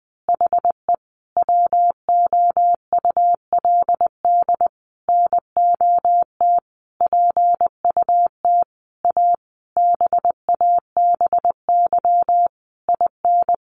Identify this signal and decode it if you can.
Morse code, heard as a keyed tone, about 20 wpm, 700 Hz. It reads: HE WOULD NOT PUT A BABY IN
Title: Easy Phrases 20wpm